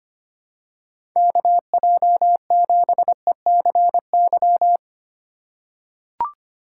Morse code, heard as a keyed tone, about 25 wpm, 700 Hz. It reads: KJ7ECY
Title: USA Call Signs - 2x Character Spacing - 25wpm